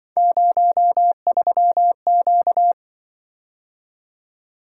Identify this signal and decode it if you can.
Morse code, heard as a keyed tone, about 24 wpm, 700 Hz. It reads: 03Q
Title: Speed-Racing - Three Letter-Number Combinations - 20wpm